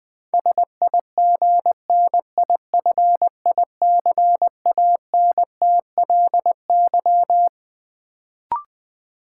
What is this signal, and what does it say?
Morse code, heard as a keyed tone, about 20 wpm, 700 Hz. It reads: SIGNIFICANTLY
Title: Sentences from Top 1500 Words - New Word Drill - 20wpm